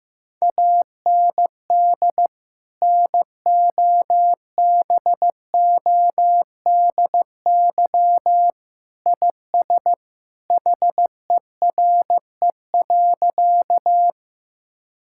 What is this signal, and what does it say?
Morse code, heard as a keyed tone, about 15 wpm, 700 Hz. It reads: AND NOBODY IS HERE.
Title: Sentences from Top 1600 Words 15wpm